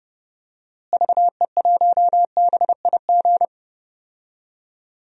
Morse code, heard as a keyed tone, about 30 wpm, 700 Hz. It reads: VE1BIG
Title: DXCC-100 Call Signs - 6 Characters - Repeated 3x - 30wpm